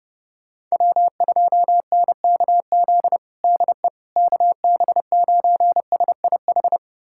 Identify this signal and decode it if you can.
Morse code, heard as a keyed tone, about 32 wpm, 700 Hz. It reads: W2NKZ DE KB9SIH
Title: USA Callsign DE Callsign 30wpm